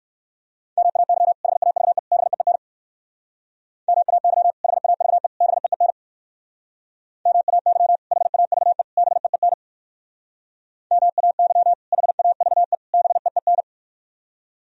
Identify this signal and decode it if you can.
Morse code, heard as a keyed tone, about 58 wpm, 700 Hz. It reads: MAY HAVE BEEN MAY HAVE BEEN MAY HAVE BEEN MAY HAVE BEEN